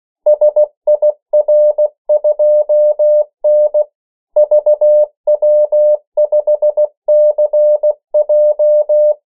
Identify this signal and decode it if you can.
Morse code, heard as a keyed tone, about 16 wpm, 600 Hz. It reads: SIR2N VW5CJ
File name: morse mp3 / 14 urok 13